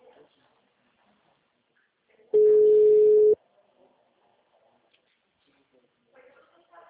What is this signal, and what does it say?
Ringback tone (Europe, 425 Hz), with speech.